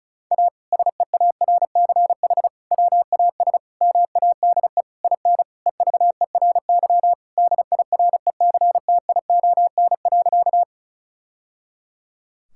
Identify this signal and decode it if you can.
Morse code, heard as a keyed tone, about 36 wpm, 700 Hz. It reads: A SEARCH WAS MADE IN EVERY DIRECTION.
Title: Sentences from Top 1000 Words - Review Part 5 - 35wpm